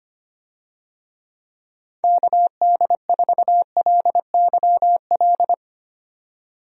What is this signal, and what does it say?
Morse code, heard as a keyed tone, about 25 wpm, 700 Hz. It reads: KD4LYL
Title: USA Call Signs - 6 Characters - 25wpm